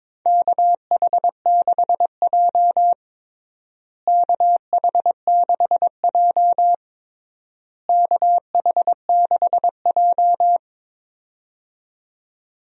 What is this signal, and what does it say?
Morse code, heard as a keyed tone, about 22 wpm, 700 Hz. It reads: KH6J KH6J KH6J
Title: USA Call Signs - 4 Characters - Repeated 3x - 22wpm